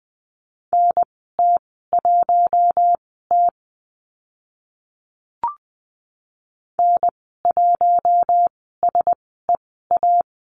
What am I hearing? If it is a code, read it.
Morse code, heard as a keyed tone, about 20 wpm, spaced as for 14 wpm, 700 Hz. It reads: NT1T N1SEA
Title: 042 Lesson 8g - Number 1 - Call Signs 20wpm